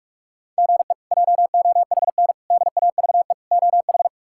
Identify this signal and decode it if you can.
Morse code, heard as a keyed tone, about 46 wpm, 700 Hz. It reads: GE JOHN DAVE OH
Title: K1USN SST QSO Practice - 45wpm